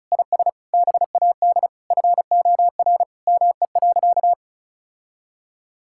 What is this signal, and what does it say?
Morse code, heard as a keyed tone, about 36 wpm, 700 Hz. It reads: IS BAD FOR ME.
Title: Sentences from Top 1600 Words 35wpm